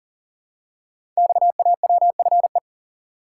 Morse code, heard as a keyed tone, about 42 wpm, 700 Hz. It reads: XAWFE
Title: Code Groups - 5 Letters - 40wpm